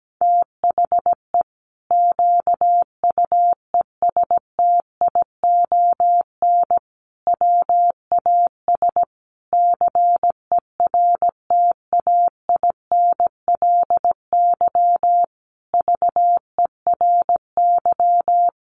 Morse code, heard as a keyed tone, about 17 wpm, 700 Hz. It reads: THE QUESTION WAS CERTAINLY VERY
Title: Sentences from Top 1200 Words 17wpm